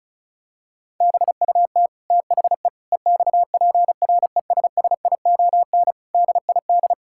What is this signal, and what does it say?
Morse code, heard as a keyed tone, about 36 wpm, 700 Hz. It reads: BUT THE EXPRESSION DID
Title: Sentences from Top 1600 Words 35wpm